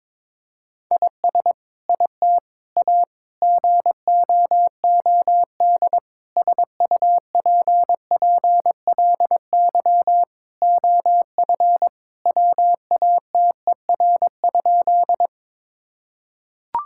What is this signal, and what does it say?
Morse code, heard as a keyed tone, about 21 wpm, 700 Hz. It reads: IS IT A GOOD SUPPLY OF WATER?
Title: Sentences from Top 1000 Words - Review Part 5 - 22wpm